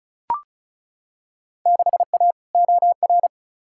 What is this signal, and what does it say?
Morse code, heard as a keyed tone, about 36 wpm, 700 Hz. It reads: 6A OR
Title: Field Day Exchanges - 35wpm